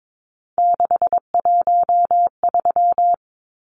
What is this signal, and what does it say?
Morse code, heard as a keyed tone, about 22 wpm, 700 Hz. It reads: 613
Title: Numbers - 3-Digit 22wpm